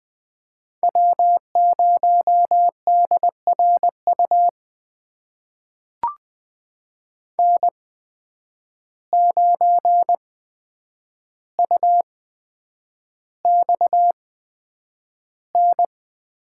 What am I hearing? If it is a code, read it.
Morse code, heard as a keyed tone, about 20 wpm, spaced as for 9 wpm, 700 Hz. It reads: W0DRU N9UXN
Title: USA Call Signs - 8x Character Spacing - 20wpm